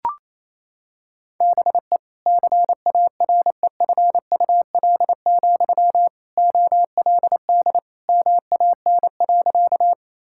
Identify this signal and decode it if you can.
Morse code, heard as a keyed tone, about 28 wpm, 700 Hz. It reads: BE CAREFUL, OLD MAN.
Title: Sentences from Top 1600 Words 28wpm